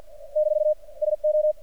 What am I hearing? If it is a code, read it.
Morse code, heard as a keyed tone, about 47 wpm, 610 Hz. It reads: - SO
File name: archived / CRSnet